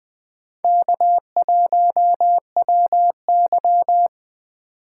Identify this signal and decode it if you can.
Morse code, heard as a keyed tone, about 23 wpm, 700 Hz. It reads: K1WY